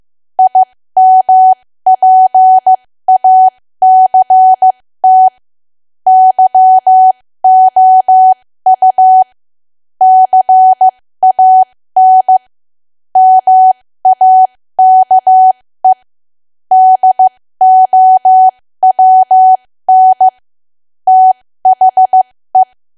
Morse code, heard as a keyed tone, about 15 wpm, spaced as for 13 wpm, 750 Hz. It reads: IMPACT YOU CAN MAKE DOWN THE
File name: cwa / W1AW-Level2 / 13WPM